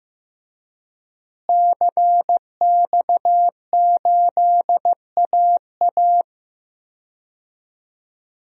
Morse code, heard as a keyed tone, about 15 wpm, 700 Hz. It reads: CX8AA
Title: DXCC-100 Call Signs - 5 Characters - Repeated 3x - 15wpm